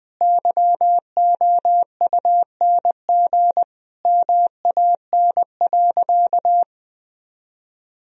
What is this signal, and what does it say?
Morse code, heard as a keyed tone, about 21 wpm, 700 Hz. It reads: YOUNG MAN.
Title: Sentences from Top 1000 Words - Review Part 4 - 20wpm